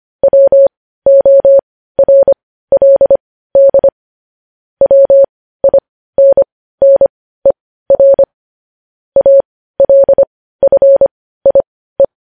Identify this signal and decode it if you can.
Morse code, heard as a keyed tone, about 25 wpm, spaced as for 15 wpm, 550 Hz. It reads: WORLD WINNER ALFIE
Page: news-headlines-in-morse-code-at-15-wpm